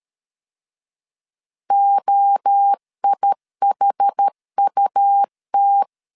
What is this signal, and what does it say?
Morse code, heard as a keyed tone, about 13 wpm, 800 Hz. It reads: OIHUT
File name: CW / DET / 1 3 1500C S B 12wpm 2828